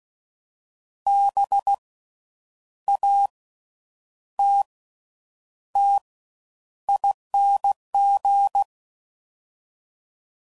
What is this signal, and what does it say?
Morse code, heard as a keyed tone, about 16 wpm, 800 Hz. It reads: B A T T ING
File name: Suffixes / Special INGs / Slow With Faster Ings